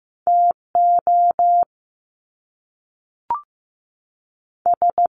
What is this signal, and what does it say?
Morse code, heard as a keyed tone, about 15 wpm, 700 Hz. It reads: TO S
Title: Top Two Letter Combinations in Words 15wpm